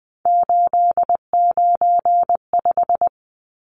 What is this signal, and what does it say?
Morse code, heard as a keyed tone, about 20 wpm, 700 Hz. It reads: 895
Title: Numbers - 3-Digit 20wpm